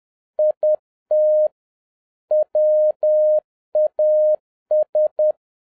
Morse code, heard as a keyed tone, about 10 wpm, 610 Hz. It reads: IT WAS
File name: 2011-07-mp3 / n56 BACK FROM A BIG ACCIDENT 10wpm